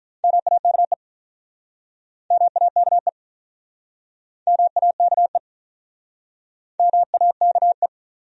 Morse code, heard as a keyed tone, about 48 wpm, 700 Hz. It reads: MAKE MAKE MAKE MAKE